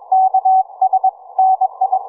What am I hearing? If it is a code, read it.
Morse code, heard as a keyed tone, about 23 wpm, 760 Hz. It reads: KS NI
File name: cw forum / ve1zj 1830 160m 04h00